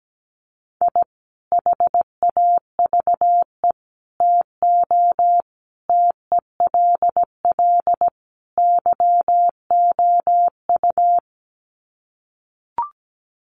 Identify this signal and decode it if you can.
Morse code, heard as a keyed tone, about 17 wpm, 700 Hz. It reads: I HAVE TO TELL YOU
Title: Sets of 5 Words 17wpm